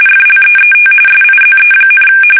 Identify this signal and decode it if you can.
RTTY, about 45.45 baud. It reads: FLEET SIZE,
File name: fsk850 / usnavy5